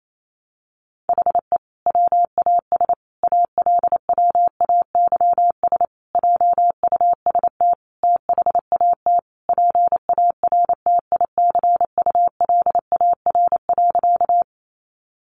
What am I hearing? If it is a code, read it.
Morse code, heard as a keyed tone, about 28 wpm, 700 Hz. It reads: HE WAS ALWAYS JUST THAT PARTICULAR.
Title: Sentences from Top 600 Words 28wpm